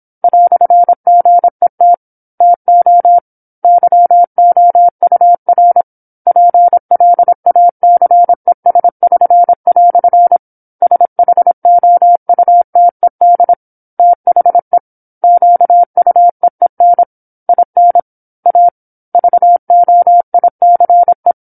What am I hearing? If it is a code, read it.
Morse code, heard as a keyed tone, about 27 wpm, 700 Hz. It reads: "GET TO YOUR PLACES<SN>" SHOUTED THE QUEEN IN A VOICE
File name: 08-Chapter-08 26 WPM